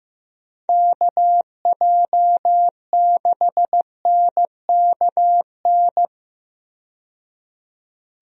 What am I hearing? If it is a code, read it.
Morse code, heard as a keyed tone, about 15 wpm, 700 Hz. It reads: KJ6NKN
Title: USA Call Signs - 6 Characters - 15wpm